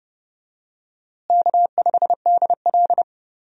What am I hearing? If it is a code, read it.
Morse code, heard as a keyed tone, about 30 wpm, 700 Hz. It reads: K5DL
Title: USA Call Signs - 4 Characters - Repeated 3x - 30wpm